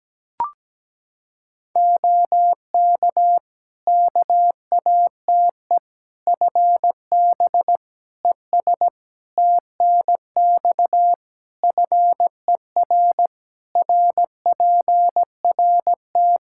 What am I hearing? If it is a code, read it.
Morse code, heard as a keyed tone, about 17 wpm, 700 Hz. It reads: OK KATE FB ES TNX FER RPRT
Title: Practice QSO - Exchange 2 - Info - 17wpm